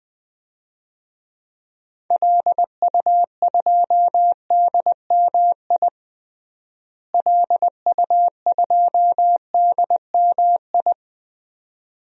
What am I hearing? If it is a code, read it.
Morse code, heard as a keyed tone, about 20 wpm, 700 Hz. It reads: LU2DMI LU2DMI
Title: DXCC-100 Call Signs - 6 Characters - Repeated 3x - 20wpm